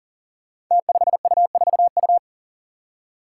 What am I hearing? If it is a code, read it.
Morse code, heard as a keyed tone, about 38 wpm, 700 Hz. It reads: T5U4U
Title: Five Challenging Letter-Number Combinations 40wpm